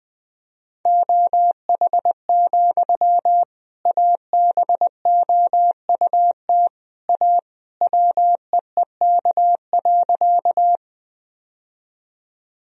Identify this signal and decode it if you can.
Morse code, heard as a keyed tone, about 20 wpm, 700 Hz. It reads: OH, ABOUT A WEEK.
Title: Sentences from Top 1000 Words - Review Part 2 - 20wpm